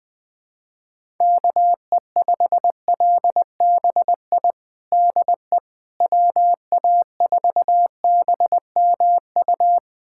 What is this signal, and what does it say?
Morse code, heard as a keyed tone, about 21 wpm, 700 Hz. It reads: KE5LBI DE WA4BMU
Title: USA Callsign DE Callsign - 6 Characters - 20wpm